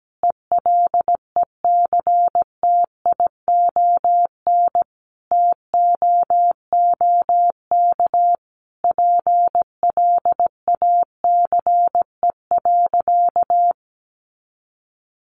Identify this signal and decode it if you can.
Morse code, heard as a keyed tone, about 17 wpm, 700 Hz. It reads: ELECTION TOOK PLACE.
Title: Sentences from Top 1500 Words 17wpm